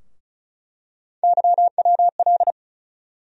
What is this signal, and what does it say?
Morse code, heard as a keyed tone, about 35 wpm, 700 Hz. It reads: YWL
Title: USA Call Sign Postfixes - 3 Letters 35wpm